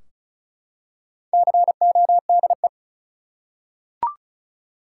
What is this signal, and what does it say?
Morse code, heard as a keyed tone, about 36 wpm, 700 Hz. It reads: CODE
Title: Four Letter Words - 35wpm